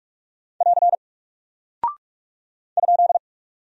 Morse code, heard as a keyed tone, about 44 wpm, 700 Hz. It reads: <AR> ?
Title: Single Character (Letter:Number:Punctuation:Prosign) Alt 45wpm